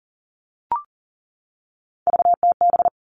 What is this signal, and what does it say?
Morse code, heard as a keyed tone, about 40 wpm, 700 Hz. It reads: VTB
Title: All Cut Numbers - 3-Digit 40wpm